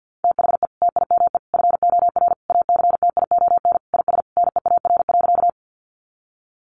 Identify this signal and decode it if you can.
Morse code, heard as a keyed tone, about 50 wpm, 700 Hz. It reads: THE TIME FOR ACTION IS NEAR.
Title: Sentences from Top 700 Words 50wpm